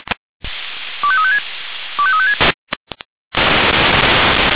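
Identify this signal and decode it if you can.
Five-tone selective call (ZVEI), heard as quick runs of five tones, tones of about 70 ms. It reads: 25356 25356
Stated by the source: ZVEI five-tone selective call received on SDR.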